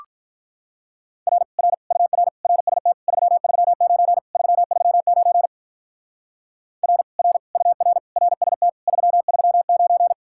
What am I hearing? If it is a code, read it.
Morse code, heard as a keyed tone, about 45 wpm, 700 Hz. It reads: R R UR RST 339 339 R R UR RST 339